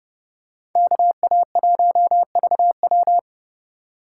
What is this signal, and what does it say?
Morse code, heard as a keyed tone, about 30 wpm, 700 Hz. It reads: KA1VW